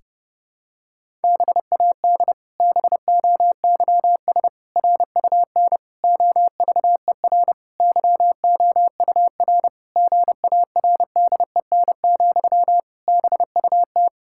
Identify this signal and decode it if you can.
Morse code, heard as a keyed tone, about 30 wpm, 700 Hz. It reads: BAD BOYS RUN OVER YOUR GARDEN, BUT
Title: Pangram Challenge v2 - 30wpm